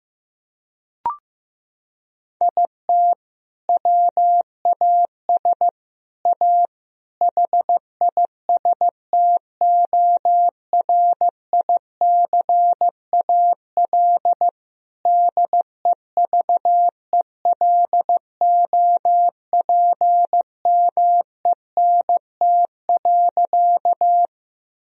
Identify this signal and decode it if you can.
Morse code, heard as a keyed tone, about 15 wpm, 700 Hz. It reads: IT WAS A HISTORICAL DEVELOPMENT.
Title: Sentences from Top 1000 Words - Review Part 4 - 15wpm